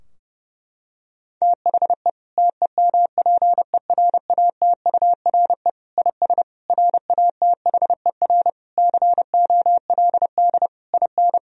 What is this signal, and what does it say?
Morse code, heard as a keyed tone, about 32 wpm, 700 Hz. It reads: THE TEMPERATURE IS RATHER COLD IN
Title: Sentences from Top 1000 Words - Review Part 4 - 30wpm